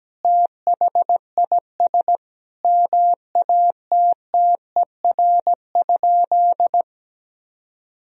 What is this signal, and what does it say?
Morse code, heard as a keyed tone, about 17 wpm, 700 Hz. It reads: THIS MATTER?
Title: Sentences from Top 1000 Words - Review Part 1 - 17wpm